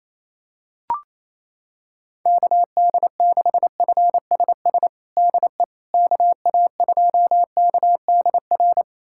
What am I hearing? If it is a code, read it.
Morse code, heard as a keyed tone, about 28 wpm, 700 Hz. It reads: KD6FSS DE KA2KDR
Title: USA Callsign DE Callsign 28wpm